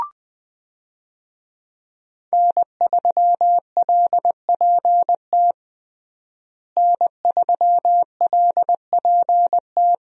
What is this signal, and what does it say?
Morse code, heard as a keyed tone, about 20 wpm, 700 Hz. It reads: N3LPT N3LPT
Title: USA Call Signs - 5 Characters - Repeated 2x - 20wpm